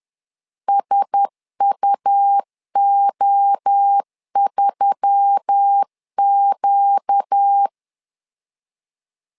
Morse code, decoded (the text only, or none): SUO3Q